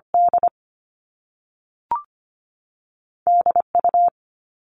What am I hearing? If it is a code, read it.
Morse code, heard as a keyed tone, about 25 wpm, 700 Hz. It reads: D DU
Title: All Cut Numbers - 2-Digit 25wpm